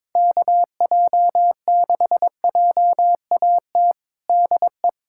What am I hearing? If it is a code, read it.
Morse code, heard as a keyed tone, about 22 wpm, 700 Hz. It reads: KJ6JAT DE
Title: USA Callsign DE Callsign - 6 Characters - 22wpm